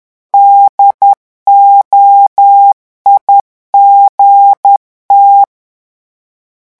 Morse code, heard as a keyed tone, about 10 wpm, 800 Hz. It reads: DOIGT
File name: morse mp3 / revision francais 10 mots minute